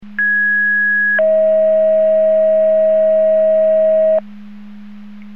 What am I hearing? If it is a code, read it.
Two-tone page, heard as a one-second tone, then a three-second tone, about 1650 then 640 Hz. It is heard from 0.2 to 4.2 s.
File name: Portage 154.385 Paging